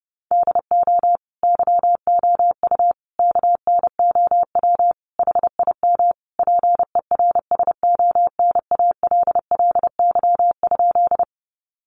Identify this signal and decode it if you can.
Morse code, heard as a keyed tone, about 30 wpm, 700 Hz. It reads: DO YOU KNOW HIM PERSONALLY?
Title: Sentences from Top 1400 Words 30wpm